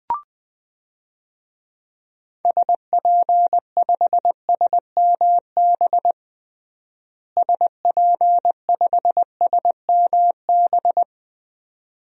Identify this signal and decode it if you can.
Morse code, heard as a keyed tone, about 20 wpm, 700 Hz. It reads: SP5SMB SP5SMB
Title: DXCC-100 Call Signs - 6 Characters - Repeated 2x - 20wpm